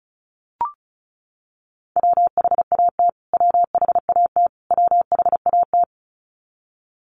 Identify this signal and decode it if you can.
Morse code, heard as a keyed tone, about 34 wpm, 700 Hz. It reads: WHAT WHAT WHAT
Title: Top 500 Words Repeated 35wpm